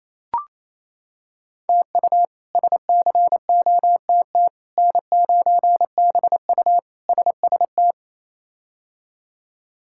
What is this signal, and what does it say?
Morse code, heard as a keyed tone, about 28 wpm, 700 Hz. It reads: TU SCOTT N9BU SST
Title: K1USN SST QSO Practice - 28wpm